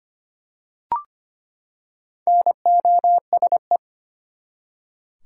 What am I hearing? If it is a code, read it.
Morse code, heard as a keyed tone, about 25 wpm, 700 Hz. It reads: NOSE
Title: Four Letter Words - 25wpm